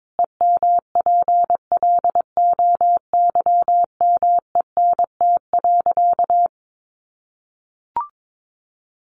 Morse code, heard as a keyed tone, about 22 wpm, 700 Hz. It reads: EMPLOYMENT.
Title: Sentences from Top 1500 Words 22wpm